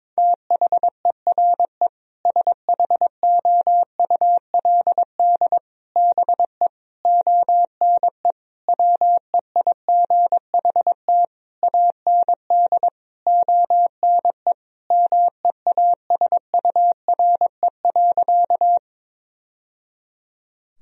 Morse code, heard as a keyed tone, about 22 wpm, 700 Hz. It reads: THERE SHOULD BE ONE WEIGHT AND ONE MEASURE.
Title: Sentences from Top 1000 Words - Review Part 3 - 22wpm